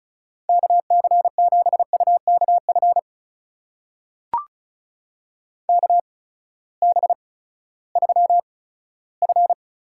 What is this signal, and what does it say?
Morse code, heard as a keyed tone, about 36 wpm, spaced as for 18 wpm, 700 Hz. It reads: KC7UKF KB3F
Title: USA Call Signs - 6 Characters - 8x Character Spacing - 35wpm